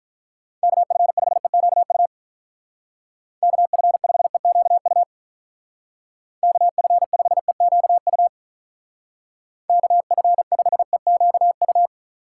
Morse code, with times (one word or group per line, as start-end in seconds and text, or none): 0.6-2.1 KF5EQU
3.4-5.0 KF5EQU
6.4-8.3 KF5EQU
9.7-11.9 KF5EQU